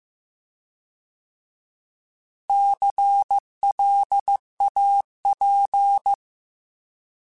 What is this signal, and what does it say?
Morse code, heard as a keyed tone, about 15 wpm, 800 Hz. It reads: CLAP
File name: Alphabetical / Alphabetical mp3s / Cs @13